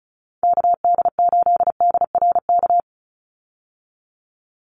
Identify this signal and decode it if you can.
Morse code, heard as a keyed tone, about 34 wpm, 700 Hz. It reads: KD8DRK
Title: USA Call Signs 35wpm